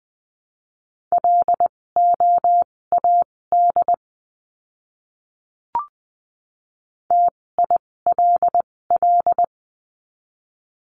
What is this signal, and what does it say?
Morse code, heard as a keyed tone, about 20 wpm, spaced as for 16 wpm, 700 Hz. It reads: LOAD TILL
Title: 256 Lesson 42a - Words 12-20wpm